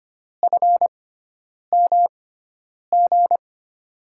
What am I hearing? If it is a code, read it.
Morse code, heard as a keyed tone, about 25 wpm, spaced as for 9 wpm, 700 Hz. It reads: FMG
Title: USA Call Signs - 6 Characters - 6x Character Spacing - 25wpm